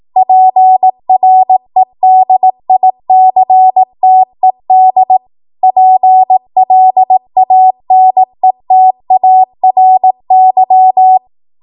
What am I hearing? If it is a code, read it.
Morse code, heard as a keyed tone, about 18 wpm, 750 Hz. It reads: PREDICTED PLANETARY
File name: Morse / 18WPM